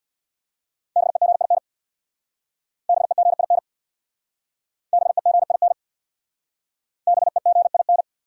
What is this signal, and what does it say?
Morse code, heard as a keyed tone, about 61 wpm, 700 Hz. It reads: BEGIN BEGIN BEGIN BEGIN
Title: Speed-Racing Top 500 Words 50wpm